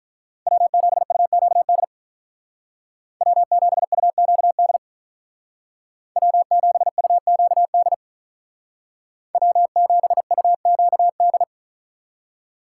Morse code, heard as a keyed tone, about 46 wpm, 700 Hz. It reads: W7UQD W7UQD W7UQD W7UQD